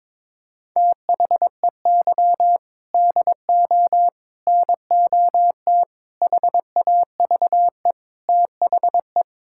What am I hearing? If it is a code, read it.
Morse code, heard as a keyed tone, about 22 wpm, 700 Hz. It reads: THEY DO NOT HAVE THE
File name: practice / 22wpm / Build A Phrase - 5-Word Phrases - 22wpm